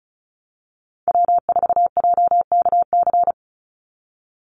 Morse code, heard as a keyed tone, about 34 wpm, 700 Hz. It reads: W4JKC
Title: NA SOTA Chaser Callsigns 35wpm